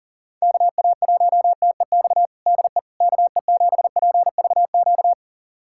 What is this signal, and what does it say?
Morse code, heard as a keyed tone, about 40 wpm, 700 Hz. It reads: KA1TEX DE KE7PVQ
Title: USA Callsign DE Callsign 40wpm